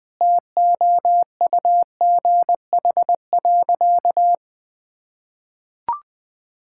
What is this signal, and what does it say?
Morse code, heard as a keyed tone, about 20 wpm, 700 Hz. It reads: TOUGH.
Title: Sentences with 7-Word Phrases - Spoken Beginning - 20wpm